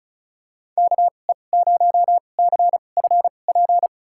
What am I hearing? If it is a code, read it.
Morse code, heard as a keyed tone, about 34 wpm, spaced as for 25 wpm, 700 Hz. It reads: KE0CFP
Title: USA Call Signs - 6 Characters - 2x Character Spacing - 35wpm